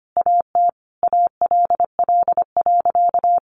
Morse code, heard as a keyed tone, about 25 wpm, 700 Hz. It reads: AT ALL.
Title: Sentences from Top 1300 Words 25wpm